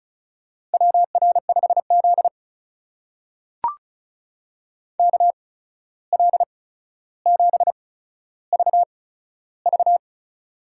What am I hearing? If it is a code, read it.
Morse code, heard as a keyed tone, about 36 wpm, spaced as for 14 wpm, 700 Hz. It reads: WR5Z KL7VV
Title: USA Call Signs - 8x Character Spacing - 35wpm